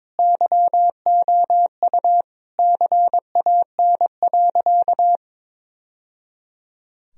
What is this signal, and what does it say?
Morse code, heard as a keyed tone, about 22 wpm, 700 Hz. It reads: YOU CAN.
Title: Sentences from Top 1000 Words - Review Part 2 - 22wpm